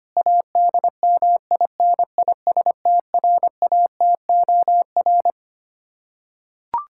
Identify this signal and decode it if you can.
Morse code, heard as a keyed tone, about 25 wpm, 700 Hz. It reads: ADMINISTRATOR
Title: K1USN SST QSO Practice 25wpm